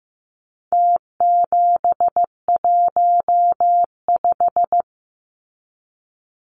Morse code, heard as a keyed tone, about 15 wpm, 700 Hz. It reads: T715
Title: Numbers - Contest Style - 4-Digit 15wpm